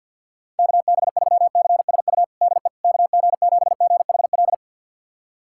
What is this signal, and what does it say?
Morse code, heard as a keyed tone, about 50 wpm, 700 Hz. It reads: KB3CSU DE KG7GHL
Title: USA Callsign DE Callsign - 6 Characters - 50wpm